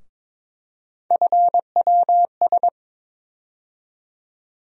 Morse code, heard as a keyed tone, about 21 wpm, 700 Hz. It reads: FWS
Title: USA Call Sign Postfixes - 3 Letters 22wpm